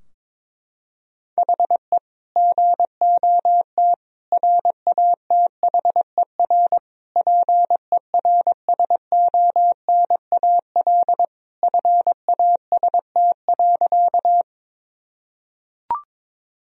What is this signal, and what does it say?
Morse code, heard as a keyed tone, about 22 wpm, 700 Hz. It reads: HE GOT RATHER PERSONAL FAST.
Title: Sentences from Top 1000 Words - Review Part 4 - 22wpm